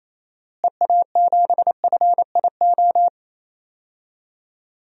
Morse code, heard as a keyed tone, about 28 wpm, 700 Hz. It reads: EA7FIO
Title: DXCC-100 Call Signs - 6 Characters - Repeated 2x - 28wpm